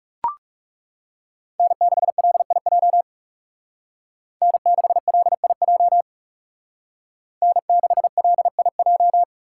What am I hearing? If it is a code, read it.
Morse code, heard as a keyed tone, about 38 wpm, 700 Hz. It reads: N6LIJ N6LIJ N6LIJ